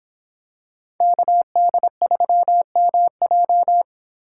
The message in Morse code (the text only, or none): KD3MJ